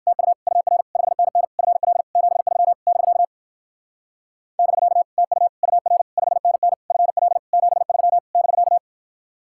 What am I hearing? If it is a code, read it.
Morse code, heard as a keyed tone, about 62 wpm, 700 Hz. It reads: TU UR 5NN FL 73 <BK> <BK> TU UR 5NN FL 73 <BK>